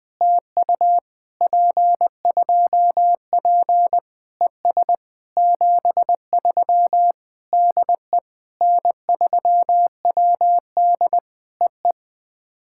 Morse code, heard as a keyed tone, about 19 wpm, 700 Hz. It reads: TU P2P ES 73 DE N3WD EE
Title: POTA QSO Practice - 20wpm